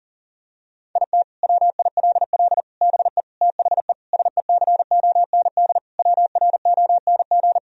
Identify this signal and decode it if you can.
Morse code, heard as a keyed tone, about 40 wpm, 700 Hz. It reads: IT WILL BE THE SECOND WRONG
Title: Sentences from Top 1000 Words - Review Part 2 - 40wpm